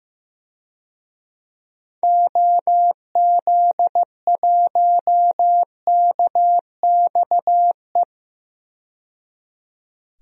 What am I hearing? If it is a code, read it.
Morse code, heard as a keyed tone, about 15 wpm, 700 Hz. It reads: OZ1KXE
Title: DXCC-100 Call Signs - 6 Characters - 15wpm